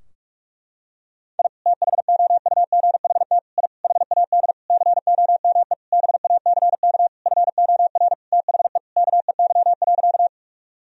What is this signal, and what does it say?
Morse code, heard as a keyed tone, about 41 wpm, 700 Hz. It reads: I THOUGHT I HAD COME BACK FOR THE KEY.
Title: Sentences from Top 1000 Words - Review Part 4 - 45wpm